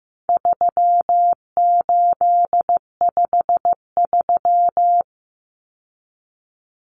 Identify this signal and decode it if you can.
Morse code, heard as a keyed tone, about 14 wpm, 700 Hz. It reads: 3853
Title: Common Cut Numbers - 4-Digit 15wpm